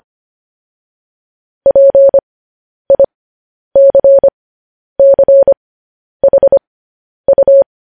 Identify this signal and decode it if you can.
Morse code, heard as a keyed tone, about 25 wpm, spaced as for 10 wpm, 550 Hz. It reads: PICCHU
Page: News Headlines in Morse Code at 10 WPM